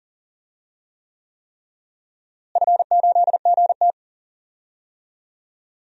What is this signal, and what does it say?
Morse code, heard as a keyed tone, about 40 wpm, 700 Hz. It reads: F8GT
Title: DXCC-100 Call Signs - 4 Characters - Repeated 3x - 40wpm